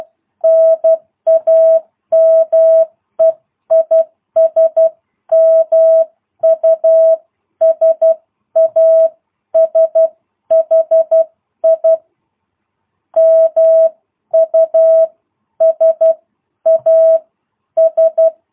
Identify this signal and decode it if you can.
Morse code, heard as a keyed tone, about 12 wpm, spaced as for 10 wpm, 650 Hz. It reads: NAME IS MUSASHI MUSAS